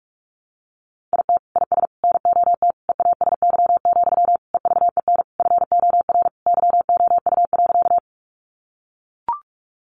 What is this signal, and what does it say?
Morse code, heard as a keyed tone, about 44 wpm, 700 Hz. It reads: IT IS NOT EASY, EVEN FOR YOU.
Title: Sentences from Top 700 Words 45wpm